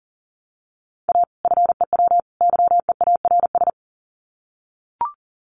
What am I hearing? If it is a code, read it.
Morse code, heard as a keyed tone, about 40 wpm, 700 Hz. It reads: A FEW YEARS
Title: Sets of 3 Words 40wpm